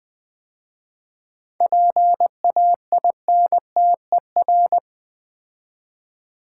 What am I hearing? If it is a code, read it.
Morse code, heard as a keyed tone, about 20 wpm, 700 Hz. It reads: PAINTER
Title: 7 Letter Words - First 3 Letters Given - 20wpm